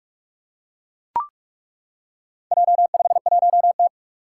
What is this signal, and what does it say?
Morse code, heard as a keyed tone, about 41 wpm, 700 Hz. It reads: J51T
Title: Four Challenging Letter-Number Combinations 45wpm